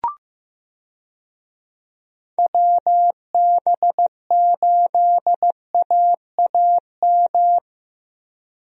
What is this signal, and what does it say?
Morse code, heard as a keyed tone, about 16 wpm, 700 Hz. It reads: WB8AAM
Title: USA Call Signs - 6 Characters - Repeated 2x - 15wpm